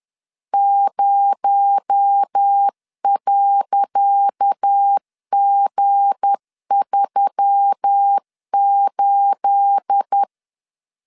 Morse code, heard as a keyed tone, about 10 wpm, 800 Hz. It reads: 0.G38
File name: LEZIONI / 10 SIGNS / 10C DOT 10wpm 2940